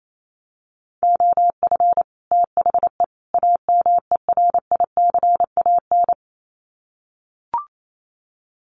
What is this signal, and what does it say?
Morse code, heard as a keyed tone, about 28 wpm, 700 Hz. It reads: OF THE AMERICAN
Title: Sets of 3 Words 28wpm